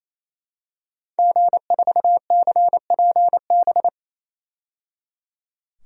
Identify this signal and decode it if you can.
Morse code, heard as a keyed tone, about 28 wpm, 700 Hz. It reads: G4CPB
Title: DXCC-100 Call Signs - 5 Characters - 28wpm